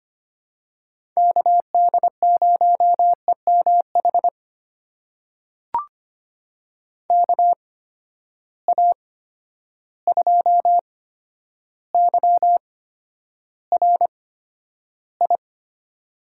Morse code, heard as a keyed tone, about 25 wpm, spaced as for 11 wpm, 700 Hz. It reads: KD0EMH KA2YRI
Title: USA Call Signs - 6 Characters - 8x Character Spacing - 25wpm